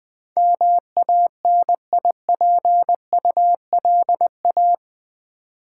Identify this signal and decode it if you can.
Morse code, heard as a keyed tone, about 20 wpm, 700 Hz. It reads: MANIPULA
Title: Build A Word - 4 Letters At A Time - 12-Letter Words - 20wpm